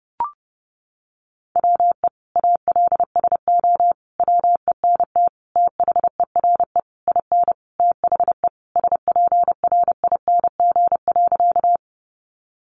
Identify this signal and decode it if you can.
Morse code, heard as a keyed tone, about 30 wpm, 700 Hz. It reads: WE ALSO WENT THERE IN THE SPRING.
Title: Sentences from Top 1200 Words 30wpm